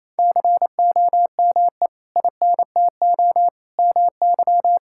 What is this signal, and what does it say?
Morse code, heard as a keyed tone, about 27 wpm, 700 Hz. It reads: COME INTO MY
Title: Sentences from Top 1000 Words - Review Part 6 - 28wpm